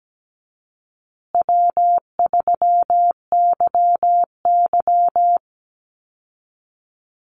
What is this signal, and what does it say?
Morse code, heard as a keyed tone, about 17 wpm, 700 Hz. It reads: W3YY
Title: Call Signs Repeated 2x - 17wpm